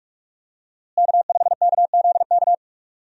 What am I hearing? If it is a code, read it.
Morse code, heard as a keyed tone, about 45 wpm, 700 Hz. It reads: K5KZK
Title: USA Call Signs - 5 Characters - Repeated 3x - 45wpm